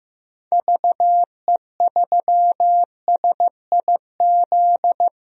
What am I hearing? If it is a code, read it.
Morse code, heard as a keyed tone, about 15 wpm, 700 Hz. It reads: VE3SIZ
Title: DXCC-100 Call Signs - 6 Characters - 15wpm